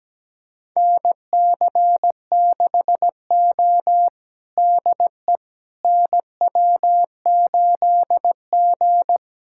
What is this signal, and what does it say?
Morse code, heard as a keyed tone, about 17 wpm, 700 Hz. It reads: NC6O DE NW8G
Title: USA Callsign DE Callsign - 4 Characters - 17wpm